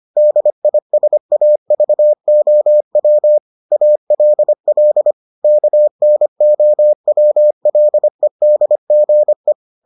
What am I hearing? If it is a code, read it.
Morse code, heard as a keyed tone, about 25 wpm, 600 Hz. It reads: DISAVOW ALL KNOWLEDGE